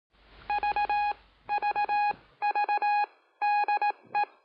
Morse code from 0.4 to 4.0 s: V V V D